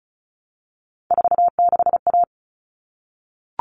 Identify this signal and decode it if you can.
Morse code, heard as a keyed tone, about 34 wpm, 700 Hz. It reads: 46A
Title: All Cut Numbers - 3-Digit 35wpm